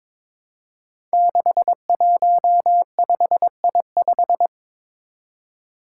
Morse code, heard as a keyed tone, about 22 wpm, 700 Hz. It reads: 615I5
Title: Five Challenging Letter-Number Combinations 22wpm